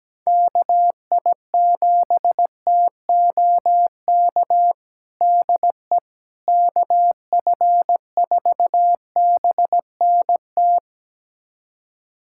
Morse code, heard as a keyed tone, about 17 wpm, 700 Hz. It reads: KI7TOK DE KF4BNT
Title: USA Callsign DE Callsign - 6 Characters - 17wpm